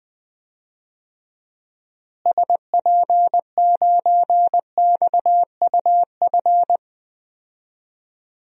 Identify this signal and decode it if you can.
Morse code, heard as a keyed tone, about 19 wpm, 700 Hz. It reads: SP9XUF
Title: DXCC-100 Call Signs - 6 Characters - Repeated 2x - 20wpm